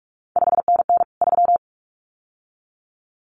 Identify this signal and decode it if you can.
Morse code, heard as a keyed tone, about 46 wpm, 700 Hz. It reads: 5NN 3
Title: CQ WW DX Contest Exchange Encore - 45wpm